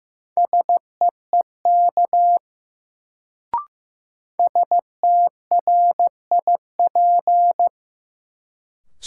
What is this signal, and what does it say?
Morse code, heard as a keyed tone, about 15 wpm, 700 Hz. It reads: SEEK STRIP
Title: Sentences from Top 1600 Words - New Word Drill - 15wpm